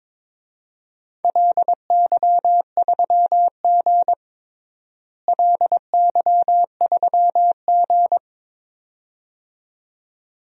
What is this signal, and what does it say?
Morse code, heard as a keyed tone, about 21 wpm, 700 Hz. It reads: LY3G LY3G
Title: DXCC-100 Call Signs - 4 Characters - Repeated 2x - 22wpm